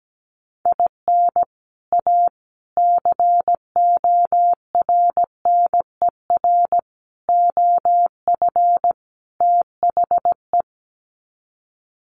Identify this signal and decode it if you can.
Morse code, heard as a keyed tone, about 17 wpm, 700 Hz. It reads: IN A CORNER OF THE
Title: Sets of 5 Words Encore2 17wpm